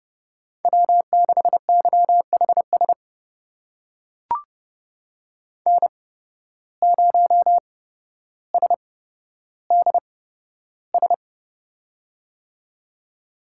Morse code, heard as a keyed tone, about 30 wpm, spaced as for 13 wpm, 700 Hz. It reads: W6YHS N0SDS
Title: USA Call Signs - 5 Characters - 8x Character Spacing - 30wpm